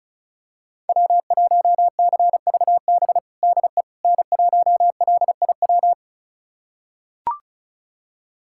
Morse code, heard as a keyed tone, about 34 wpm, 700 Hz. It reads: W1CVB DE N1LIW
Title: USA Callsign DE Callsign - 5 Characters - 35wpm